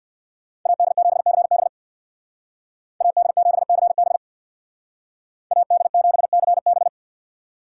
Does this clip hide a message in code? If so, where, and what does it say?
Yes: Morse code from 0.7 to 6.9 s, heard as a keyed tone, about 63 wpm, 700 Hz. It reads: AD7CB AD7CB AD7CB